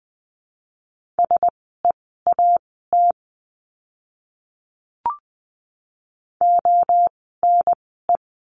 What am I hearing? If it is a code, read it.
Morse code, heard as a keyed tone, about 20 wpm, spaced as for 14 wpm, 700 Hz. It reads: SEAT ONE
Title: 036 Lesson 7f - Letter s - Words 20wpm